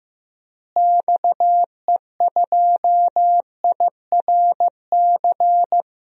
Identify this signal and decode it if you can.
Morse code, heard as a keyed tone, about 15 wpm, 700 Hz. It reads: XE2IRC
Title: DXCC-100 Call Signs - 6 Characters - Repeated 3x - 15wpm